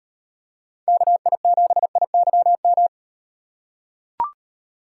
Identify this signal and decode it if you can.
Morse code, heard as a keyed tone, about 38 wpm, 700 Hz. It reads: KI7IYM